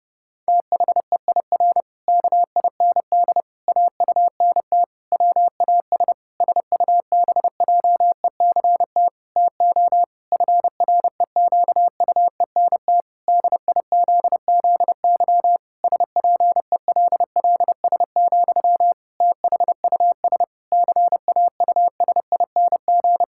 Morse code, heard as a keyed tone, about 30 wpm, 700 Hz. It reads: THEIR KIND AUNT WAS SUBJECT TO FREQUENT DIZZY SPELLS, THUS CAUSING
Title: Pangram Challenge v2 - 30wpm